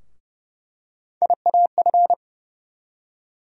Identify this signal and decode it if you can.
Morse code, heard as a keyed tone, about 31 wpm, 700 Hz. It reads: IAF
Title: USA Call Sign Postfixes - 3 Letters 30wpm